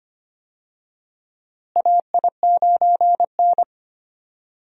Morse code, heard as a keyed tone, about 25 wpm, 700 Hz. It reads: AI9N
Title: USA Call Signs - 4 Characters - Repeated 3x - 25wpm